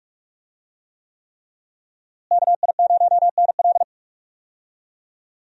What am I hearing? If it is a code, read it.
Morse code, heard as a keyed tone, about 46 wpm, 700 Hz. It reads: KI0NL
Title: USA Call Signs - 5 Characters - Repeated 3x - 45wpm